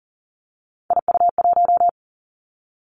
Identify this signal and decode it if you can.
Morse code, heard as a keyed tone, about 39 wpm, 700 Hz. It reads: IU1
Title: Three Challenging Letter-Number Combinations 40wpm